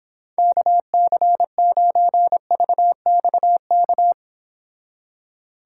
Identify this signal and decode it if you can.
Morse code, heard as a keyed tone, about 26 wpm, 700 Hz. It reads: KC9VXK